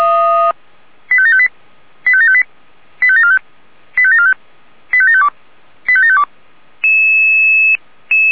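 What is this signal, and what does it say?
Five-tone selective call (ZVEI), heard as quick runs of five tones, tones of about 80 ms. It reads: 85758 85758 85745 85745 85752 85752
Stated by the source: ZVEI five-tone selective call received on SDR.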